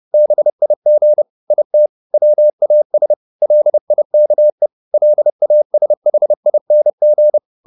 Morse code, heard as a keyed tone, about 30 wpm, 600 Hz. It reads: BIG IT WAS LIKE LASHING